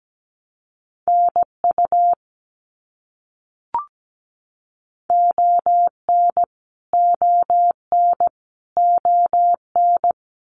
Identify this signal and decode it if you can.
Morse code, heard as a keyed tone, about 17 wpm, 700 Hz. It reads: NU ON ON ON
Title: Canadian Provinces and Territories - Repeated 3x - 17wpm